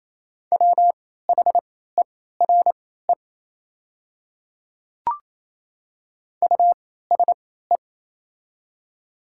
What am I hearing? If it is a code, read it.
Morse code, heard as a keyed tone, about 28 wpm, spaced as for 16 wpm, 700 Hz. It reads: WHERE USE
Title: 3x Character Spacing - Top 300 Words - 28wpm